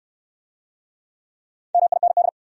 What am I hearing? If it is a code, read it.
Morse code, heard as a keyed tone, about 68 wpm, 700 Hz. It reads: MIND